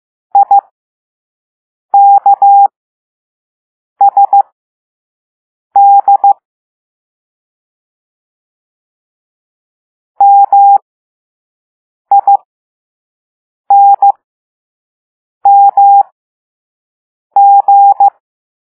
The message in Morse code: IKSD MINMG